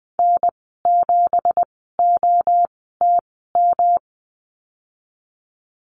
Morse code, heard as a keyed tone, about 20 wpm, spaced as for 15 wpm, 700 Hz. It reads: N7OTM
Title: 193 Lesson 31g - Letter v - Call Signs 20wpm